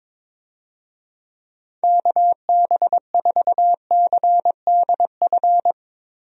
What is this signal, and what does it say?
Morse code, heard as a keyed tone, about 22 wpm, 700 Hz. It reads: KB4CDF
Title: USA Call Signs - 6 Characters - Repeated 2x - 22wpm